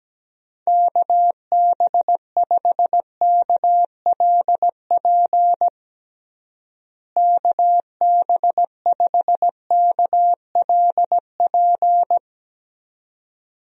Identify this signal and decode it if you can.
Morse code, heard as a keyed tone, about 17 wpm, 700 Hz. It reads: KB5KLP KB5KLP
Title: USA Call Signs - 6 Characters - Repeated 2x - 17wpm